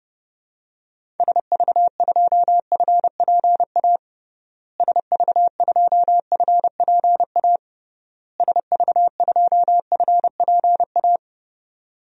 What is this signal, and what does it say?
Morse code, heard as a keyed tone, about 30 wpm, 700 Hz. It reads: SV2FPA SV2FPA SV2FPA
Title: DXCC-100 Call Signs - 6 Characters - Repeated 3x - 30wpm